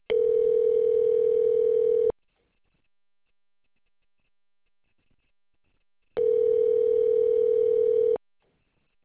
Ringback tone (North America), 440 + 480 Hz.